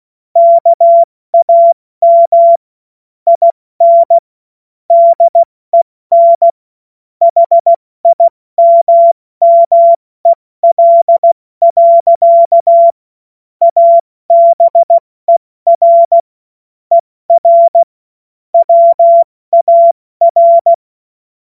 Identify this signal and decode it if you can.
Morse code, heard as a keyed tone, about 16 wpm, 680 Hz. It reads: KAM IN DEN HIMMEL. ABER ER WAR
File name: Morsekurs / Witze / Witz 25. Oktober 2022 70 F80